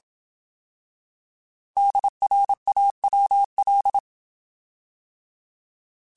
Morse code, heard as a keyed tone, about 27 wpm, 800 Hz. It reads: DRAWL